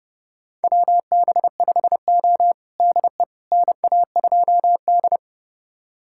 Morse code, heard as a keyed tone, about 30 wpm, 700 Hz. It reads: WB5O DE NA2D
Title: USA Callsign DE Callsign - 4 Characters - 30wpm